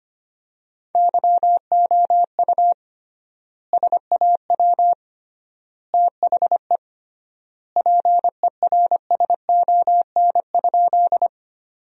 Morse code, heard as a keyed tone, about 25 wpm, 700 Hz. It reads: YOU SAW THE PERSON?
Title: 3x Word-Spacing - Sentences from Top 400 Words - 25wpm